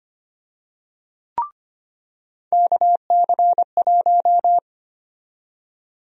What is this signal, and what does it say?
Morse code, heard as a keyed tone, about 25 wpm, 700 Hz. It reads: KC1
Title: USA Call Sign Prefixes - 25wpm